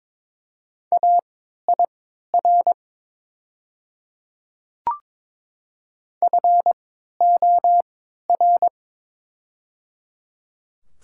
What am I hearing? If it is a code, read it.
Morse code, heard as a keyed tone, about 22 wpm, spaced as for 12 wpm, 700 Hz. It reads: AIR FOR
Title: 3x Character Spacing - Top 200 Words - 22wpm